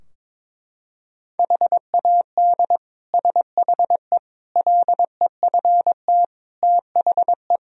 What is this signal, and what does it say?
Morse code, heard as a keyed tone, about 22 wpm, 700 Hz. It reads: HAD SHE LEFT THE
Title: Sentences from Top 1000 Words - Review Part 3 - 22wpm